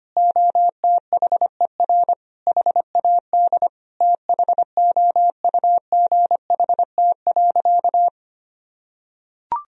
Morse code, heard as a keyed tone, about 25 wpm, 700 Hz. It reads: OTHER HAD THOUGHT.
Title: Sentences from Top 1000 Words - Review Part 1 - 25wpm